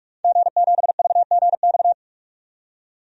Morse code, heard as a keyed tone, about 46 wpm, 700 Hz. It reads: G7VGX
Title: DXCC-100 Call Signs - 5 Characters - 45wpm